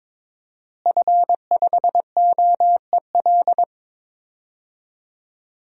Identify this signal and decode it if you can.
Morse code, heard as a keyed tone, about 22 wpm, 700 Hz. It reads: F5OEL
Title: DXCC-100 Call Signs - 5 Characters - Repeated 2x - 22wpm